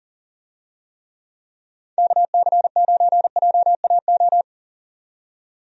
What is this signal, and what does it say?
Morse code, heard as a keyed tone, about 40 wpm, 700 Hz. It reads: KC9JAO
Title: USA Call Signs - 6 Characters - 40wpm